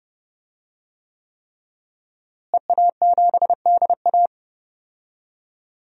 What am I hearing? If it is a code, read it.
Morse code, heard as a keyed tone, about 30 wpm, 700 Hz. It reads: EA7DA
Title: DXCC-100 Call Signs - 5 Characters - Repeated 3x - 30wpm